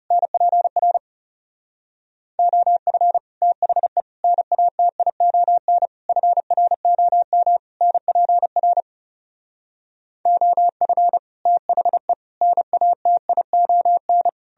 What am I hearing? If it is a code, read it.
Morse code, heard as a keyed tone, about 40 wpm, 700 Hz. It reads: NPR OF THE NATION FROM NPR OF THE NATION